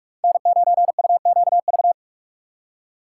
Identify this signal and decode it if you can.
Morse code, heard as a keyed tone, about 44 wpm, 700 Hz. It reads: N9UQV